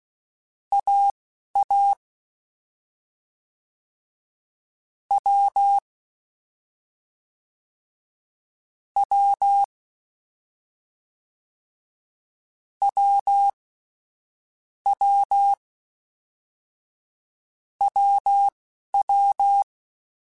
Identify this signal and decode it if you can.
Morse code, heard as a keyed tone, about 16 wpm, spaced as for 11 wpm, 800 Hz. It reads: AA W W W W WW
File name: Difficult Combos / Difficult Combos-mp3s / AWJ ShortRun @10